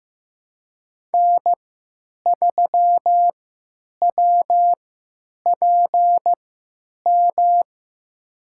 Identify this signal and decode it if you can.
Morse code, heard as a keyed tone, about 15 wpm, spaced as for 9 wpm, 700 Hz. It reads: N3WPM
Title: USA Call Signs - 5 Characters - 3x Character Spacing - 15wpm